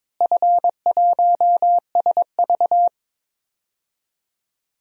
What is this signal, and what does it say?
Morse code, heard as a keyed tone, about 22 wpm, 700 Hz. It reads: F1SV
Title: DXCC-100 Call Signs - 4 Characters - Repeated 3x - 22wpm